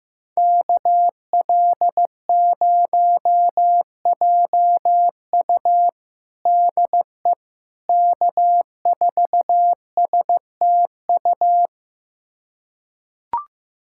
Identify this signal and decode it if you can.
Morse code, heard as a keyed tone, about 15 wpm, 700 Hz. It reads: KL0JU DE K4STU
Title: USA Callsign DE Callsign - 5 Characters - 15wpm